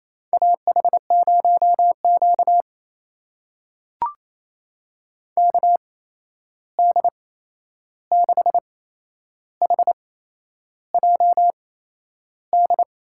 Morse code, heard as a keyed tone, about 28 wpm, spaced as for 11 wpm, 700 Hz. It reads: AH0Q KD6HJD
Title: USA Call Signs - 8x Character Spacing - 28wpm